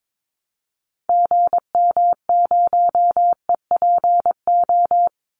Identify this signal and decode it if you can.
Morse code, heard as a keyed tone, about 22 wpm, 700 Hz. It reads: GM0EPO
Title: Call Signs Repeated 2x - 22wpm